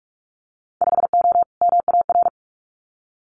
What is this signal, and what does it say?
Morse code, heard as a keyed tone, about 46 wpm, 700 Hz. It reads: 5O MAR
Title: Winter Field Day Exchanges Encore 45wpm